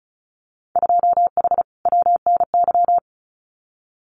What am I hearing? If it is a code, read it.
Morse code, heard as a keyed tone, about 34 wpm, 700 Hz. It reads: 2H WNY
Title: Winter Field Day Exchanges Encore 35wpm